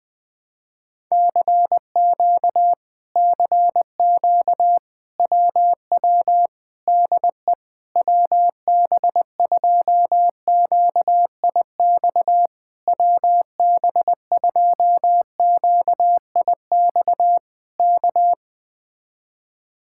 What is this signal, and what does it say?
Morse code, heard as a keyed tone, about 20 wpm, 700 Hz. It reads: CQ CQ WW DE WB2QIX WB2QIX K
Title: Speed-Racing Common QSO Phrases - 17wpm